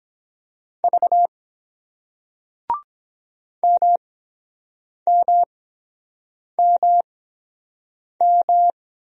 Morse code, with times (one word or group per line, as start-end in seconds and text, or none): 0.8-1.3 V
3.6-8.7 MMMM